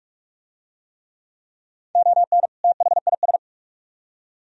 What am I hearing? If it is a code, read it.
Morse code, heard as a keyed tone, about 44 wpm, 700 Hz. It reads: ON THIS